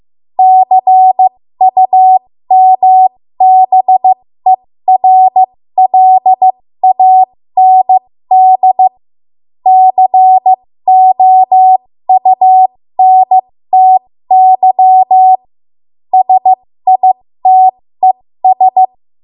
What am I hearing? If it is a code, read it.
Morse code, heard as a keyed tone, about 15 wpm, spaced as for 13 wpm, 750 Hz. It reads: CUMBERLAND COUNTY SITES
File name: Morse / 13WPM